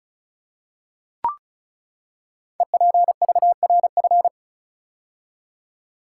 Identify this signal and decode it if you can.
Morse code, heard as a keyed tone, about 36 wpm, 700 Hz. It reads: EPVRF
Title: Code Groups - 5 Letters - 35wpm